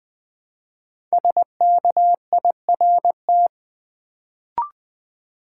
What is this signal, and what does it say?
Morse code, heard as a keyed tone, about 20 wpm, 700 Hz. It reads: SKIRT
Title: Build A Word - 1 Letter At A Time - 5-Letter Words - 20wpm